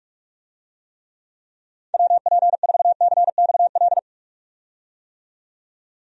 Morse code, heard as a keyed tone, about 46 wpm, 700 Hz. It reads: WP4CXL
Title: USA Call Signs - 6 Characters - 45wpm